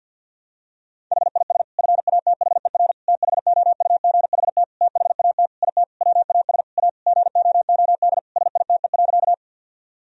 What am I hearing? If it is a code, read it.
Morse code, heard as a keyed tone, about 50 wpm, 700 Hz. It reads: HIS FATHER THOUGHT THAT IT WAS A GOOD SITE.
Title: Sentences from Top 1000 Words - Review Part 4 - 50wpm